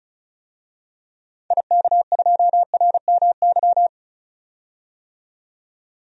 Morse code, heard as a keyed tone, about 36 wpm, 700 Hz. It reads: IK2RMY
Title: DXCC-100 Call Signs - 6 Characters - 35wpm